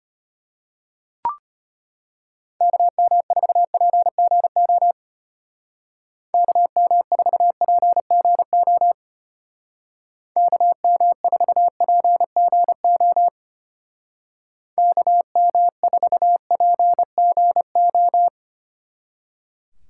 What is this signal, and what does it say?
Morse code, heard as a keyed tone, about 32 wpm, 700 Hz. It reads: KM4PGO KM4PGO KM4PGO KM4PGO